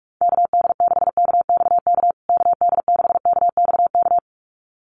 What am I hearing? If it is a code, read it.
Morse code, heard as a keyed tone, about 44 wpm, 700 Hz. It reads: KD6KXK KD6KXK
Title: USA Call Signs Repeated 2x 45wpm